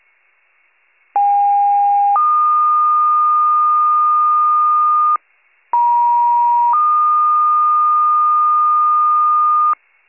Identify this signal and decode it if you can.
Two-tone page, heard as a one-second tone, then a three-second tone, about 810 then 1220 Hz, about 950 then 1220 Hz. It is heard from 1.2 to 9.7 s.